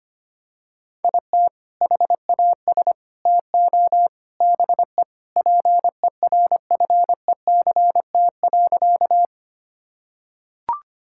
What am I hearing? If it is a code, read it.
Morse code, heard as a keyed tone, about 25 wpm, 700 Hz. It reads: IT HAS TO BE PERFECT.
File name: practice / 25wpm / Sentences with 4-Word Phrases - 25wpm